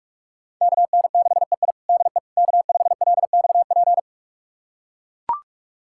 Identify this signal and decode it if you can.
Morse code, heard as a keyed tone, about 42 wpm, 700 Hz. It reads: KN6EI DE K5LXP
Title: USA Callsign DE Callsign - 5 Characters - 45wpm